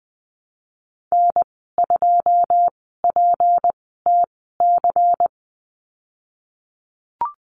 Morse code, heard as a keyed tone, about 20 wpm, spaced as for 15 wpm, 700 Hz. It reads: N2PTC
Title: 173 Lesson 28g - Number 9 - Call Signs 20wpm